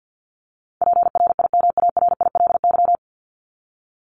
Morse code, heard as a keyed tone, about 50 wpm, 700 Hz. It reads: PRIMARILY
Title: Top 1001-1250 Words 50wpm